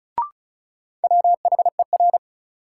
Morse code, heard as a keyed tone, about 36 wpm, 700 Hz. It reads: WHER
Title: Sentences from Top 1500 Words - New Word Drill - 35wpm